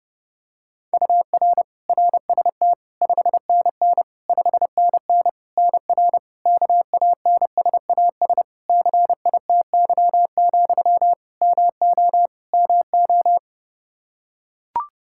Morse code, heard as a keyed tone, about 30 wpm, 700 Hz. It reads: UR RST 5NN 5NN NR KANSAS CITY, MO MO
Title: Practice QSO - Exchange 1 - Beginning - 30wpm